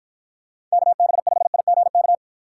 Morse code, heard as a keyed tone, about 53 wpm, 700 Hz. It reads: KB5IGK